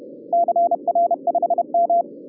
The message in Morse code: CR5M